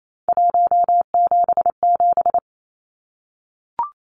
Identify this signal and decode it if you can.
Morse code, heard as a keyed tone, about 28 wpm, 700 Hz. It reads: 177
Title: Numbers - 3-Digit 28wpm